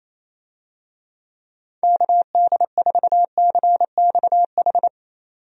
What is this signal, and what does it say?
Morse code, heard as a keyed tone, about 28 wpm, 700 Hz. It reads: KD4CXH
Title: USA Call Signs - 6 Characters - 28wpm